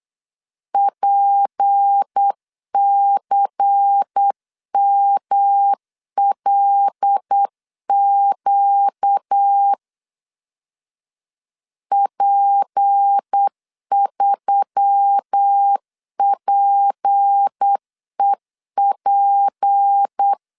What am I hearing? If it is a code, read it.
Morse code, heard as a keyed tone, about 8 wpm, 800 Hz. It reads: PCMLQ P3PEP
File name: LEZIONI / 6 PKX0 / 6 P 8wpm 3025